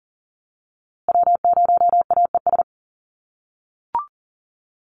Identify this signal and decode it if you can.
Morse code, heard as a keyed tone, about 40 wpm, 700 Hz. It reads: W0AES
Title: USA Call Signs Repeated 3x 40wpm